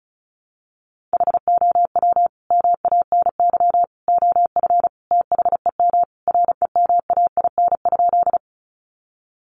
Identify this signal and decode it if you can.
Morse code, heard as a keyed tone, about 34 wpm, 700 Hz. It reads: HOW MANY OF THEM REMAIN?
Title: Sentences from Top 1200 Words 35wpm